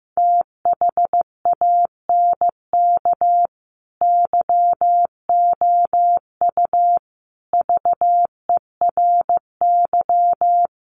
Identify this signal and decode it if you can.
Morse code, heard as a keyed tone, about 15 wpm, 700 Hz. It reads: THANK YOU VERY
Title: Sets of 5 Words 15wpm